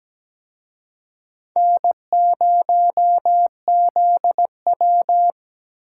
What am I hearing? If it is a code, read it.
Morse code, heard as a keyed tone, about 20 wpm, 700 Hz. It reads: N0ZW
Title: Speed Racing - USA 1x2 Call Signs 17wpm